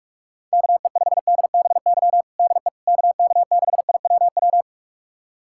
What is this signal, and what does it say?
Morse code, heard as a keyed tone, about 47 wpm, 700 Hz. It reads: KE5DBY DE KK6IWW
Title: USA Callsign DE Callsign - 6 Characters - 45wpm